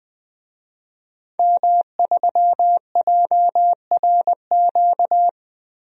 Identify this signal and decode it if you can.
Morse code, heard as a keyed tone, about 21 wpm, 700 Hz. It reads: M3JRQ
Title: DXCC-100 Call Signs - 5 Characters - Repeated 2x - 20wpm